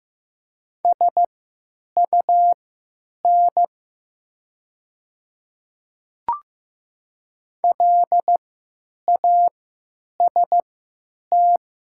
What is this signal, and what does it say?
Morse code, heard as a keyed tone, about 15 wpm, spaced as for 9 wpm, 700 Hz. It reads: SUN LAST
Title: 3x Character Spacing - Top 300 Words - 15wpm